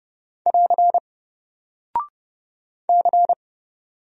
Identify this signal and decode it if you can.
Morse code, heard as a keyed tone, about 30 wpm, 700 Hz. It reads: <AR> C
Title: Single Character (Letter:Number:Punctuation:Prosign) 30wpm